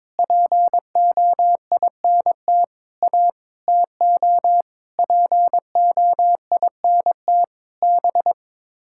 Morse code, heard as a keyed tone, about 22 wpm, 700 Hz. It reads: POINT A TO POINT B
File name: practice / 22wpm / Sets of 7 Words - 22wpm